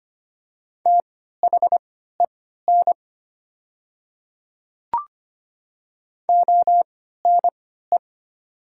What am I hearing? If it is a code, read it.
Morse code, heard as a keyed tone, about 25 wpm, spaced as for 14 wpm, 700 Hz. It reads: THEN ONE
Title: 3x Character Spacing - Top 100 Words - 25wpm